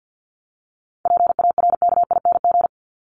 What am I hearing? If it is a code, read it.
Morse code, heard as a keyed tone, about 50 wpm, 700 Hz. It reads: PARKING
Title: Top 1501-1750 Words 50wpm